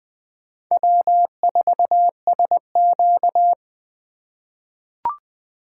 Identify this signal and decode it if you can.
Morse code, heard as a keyed tone, about 20 wpm, 700 Hz. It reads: W4SQ
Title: USA Call Signs - 4 Characters - 3x Character Spacing - 20wpm